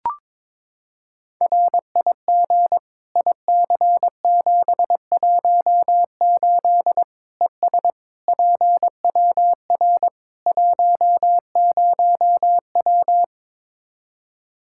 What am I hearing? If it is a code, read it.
Morse code, heard as a keyed tone, about 22 wpm, 700 Hz. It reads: RIG IC718 ES PWR 10W
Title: Practice QSO - Exchange 2 - Info - 22wpm